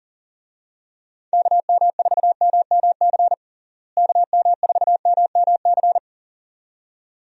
Morse code, heard as a keyed tone, about 40 wpm, 700 Hz. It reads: KM4MMC KM4MMC
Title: USA Call Signs - 6 Characters - Repeated 2x - 40wpm